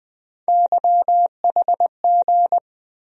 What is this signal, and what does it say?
Morse code, heard as a keyed tone, about 20 wpm, 700 Hz. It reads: YHG
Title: USA Call Sign Postfixes - 3 Letters 20wpm